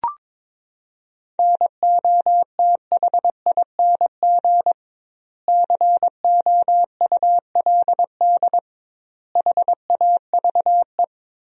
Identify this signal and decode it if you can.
Morse code, heard as a keyed tone, about 22 wpm, 700 Hz. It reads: NOTHING COULD HAVE
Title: 2x Word-Spacing - Sentences from Top 500 Words - 22wpm